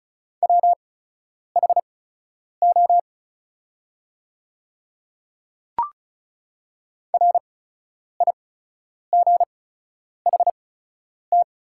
Morse code, heard as a keyed tone, about 36 wpm, spaced as for 10 wpm, 700 Hz. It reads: WHO RIGHT
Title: 8x Character Spacing - Top 200 Words - 35wpm